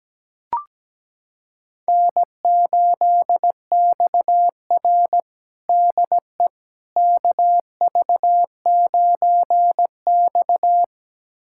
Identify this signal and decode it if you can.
Morse code, heard as a keyed tone, about 17 wpm, 700 Hz. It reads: N8XR DE KV9X
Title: USA Callsign DE Callsign - 4 Characters - 17wpm